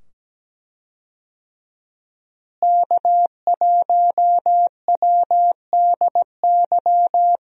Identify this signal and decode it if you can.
Morse code, heard as a keyed tone, about 17 wpm, 700 Hz. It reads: K1WDY
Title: USA Call Signs - 5 Characters - Repeated 2x - 17wpm